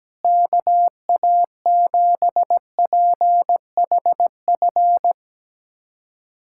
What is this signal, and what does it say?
Morse code, heard as a keyed tone, about 17 wpm, 700 Hz. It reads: KA7PHF
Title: USA Callsign DE Callsign 17wpm